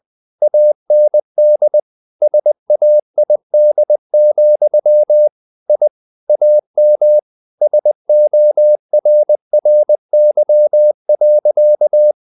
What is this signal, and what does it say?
Morse code, heard as a keyed tone, about 20 wpm, 600 Hz. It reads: AND SAID, I AM SORRY.
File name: the old man and the sea / the old man and the sea part 1 20wpm